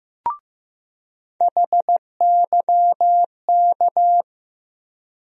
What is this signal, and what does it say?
Morse code, heard as a keyed tone, about 15 wpm, 700 Hz. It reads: HYK
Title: USA Call Sign Postfixes - 3 Letters 15wpm